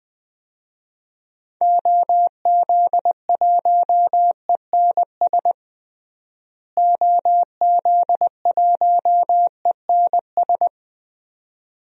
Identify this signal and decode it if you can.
Morse code, heard as a keyed tone, about 20 wpm, 700 Hz. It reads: OZ1ENS OZ1ENS
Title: DXCC-100 Call Signs - 6 Characters - Repeated 2x - 20wpm